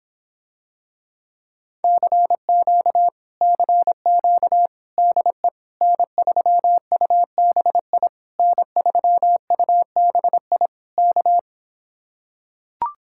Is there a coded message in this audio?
Yes: Morse code from 1.8 to 11.4 s, heard as a keyed tone, about 26 wpm, 700 Hz. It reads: CQ CQ DE N3UBI N3UBI K